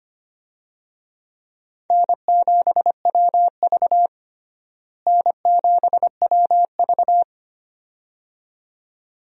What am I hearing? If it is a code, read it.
Morse code, heard as a keyed tone, about 25 wpm, 700 Hz. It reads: N7WV N7WV
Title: USA Call Signs - 4 Characters - Repeated 2x - 25wpm